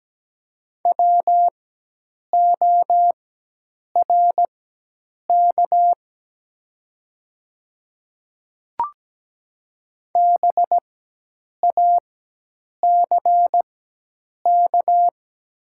Morse code, heard as a keyed tone, about 17 wpm, spaced as for 8 wpm, 700 Hz. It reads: WORK BACK
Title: 4x Character Spacing - Top 200 Words - 17wpm